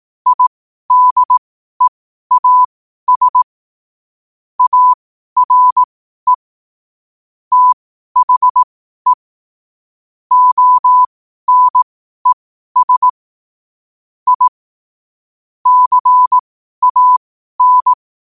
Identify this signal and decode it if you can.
Morse code, heard as a keyed tone, about 18 wpm, spaced as for 12 wpm, 1000 Hz. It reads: IDEAS ARE THE ONES I CAN
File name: qotdfiles / 12wpm